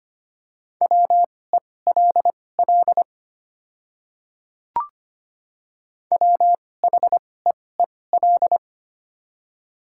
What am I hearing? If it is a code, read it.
Morse code, heard as a keyed tone, about 25 wpm, spaced as for 18 wpm, 700 Hz. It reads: WELL WHEEL
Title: 2x Character Spacing - Top 500 Words - 25wpm